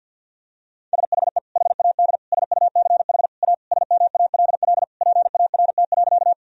Morse code, heard as a keyed tone, about 48 wpm, 700 Hz. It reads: SHE HAD SUCH A SMALL PART.
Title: Sentences from Top 1000 Words - Review Part 1 - 50wpm